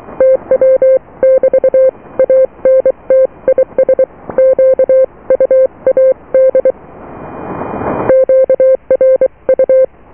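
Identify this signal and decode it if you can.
Morse code, heard as a keyed tone, about 23 wpm, 520 Hz. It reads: TW <BT> ANT IS QUAD QRU
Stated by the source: Morse QSO on 14 MHz with a German station, received at RST 579.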